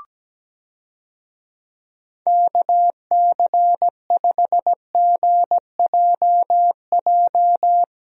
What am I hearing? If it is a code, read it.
Morse code, heard as a keyed tone, about 17 wpm, 700 Hz. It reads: KC5GJJ
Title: USA Call Signs - 6 Characters - Repeated 2x - 17wpm